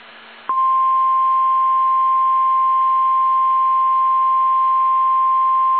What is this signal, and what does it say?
Emergency Alert System (SAME) tones (the 1050 Hz attention tone). It is heard from 0.9 to 5.3 s.
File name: eas / KSDZKDJL / KSDZKDJL